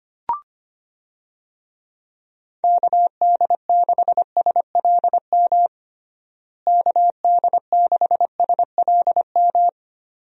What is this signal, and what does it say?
Morse code, heard as a keyed tone, about 25 wpm, 700 Hz. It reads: KD6SLM KD6SLM
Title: USA Call Signs - 6 Characters - Repeated 2x - 25wpm